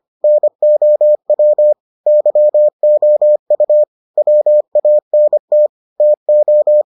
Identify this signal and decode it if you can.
Morse code, heard as a keyed tone, about 25 wpm, 600 Hz. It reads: NOW YOU WANT TO